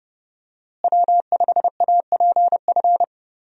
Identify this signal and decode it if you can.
Morse code, heard as a keyed tone, about 30 wpm, 700 Hz. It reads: W5APF
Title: USA Call Signs - 5 Characters - Repeated 3x - 30wpm